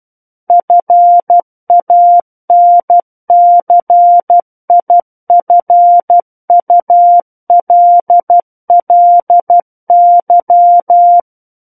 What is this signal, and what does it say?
Morse code, heard as a keyed tone, about 12 wpm, 700 Hz. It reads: FANCIFULLY
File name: Book One / 03-Chapter 3 - 12 WPM